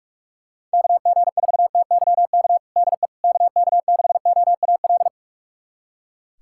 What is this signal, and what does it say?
Morse code, heard as a keyed tone, about 45 wpm, 700 Hz. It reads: KG4TYK DE KK6OAL
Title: USA Callsign DE Callsign - 6 Characters - 45wpm